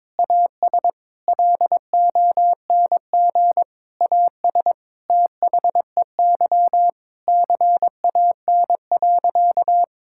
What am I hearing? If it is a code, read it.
Morse code, heard as a keyed tone, about 22 wpm, 700 Hz. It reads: AS LONG AS THEY CAN.
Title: Sentences from Top 1000 Words - Review Part 5 - 22wpm